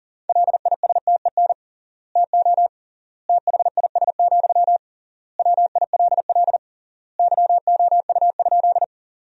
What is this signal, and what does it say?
Morse code, heard as a keyed tone, about 40 wpm, 700 Hz. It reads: LISTEN TO THIS, WILL YOU?
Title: 3x Word-Spacing - Sentences from Top 400 Words - 40wpm